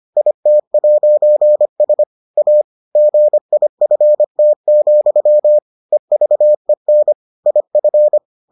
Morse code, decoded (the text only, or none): IT'S A GIFT, EVEN IF